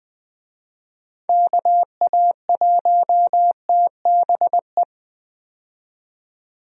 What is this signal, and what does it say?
Morse code, heard as a keyed tone, about 20 wpm, 700 Hz. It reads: KA1TBE
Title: USA Call Signs - 6 Characters - 20wpm